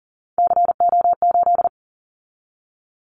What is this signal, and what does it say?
Morse code, heard as a keyed tone, about 40 wpm, 700 Hz. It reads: CO8
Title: Callsign Prefixes for Top 100 DXCC - 40wpm